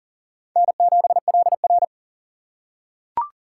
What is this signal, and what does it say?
Morse code, heard as a keyed tone, about 40 wpm, 700 Hz. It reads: N7LR
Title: K1USN SST QSO Practice - 40wpm